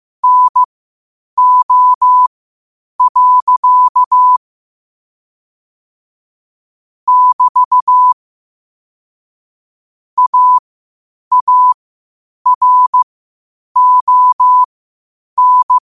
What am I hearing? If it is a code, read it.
Morse code, heard as a keyed tone, about 15 wpm, spaced as for 8 wpm, 1000 Hz. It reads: NO. <BT> AARON